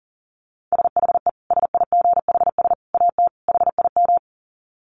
Morse code, heard as a keyed tone, about 40 wpm, 700 Hz. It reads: SHE SIGHS AT HIM
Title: Dit Challenge 40wpm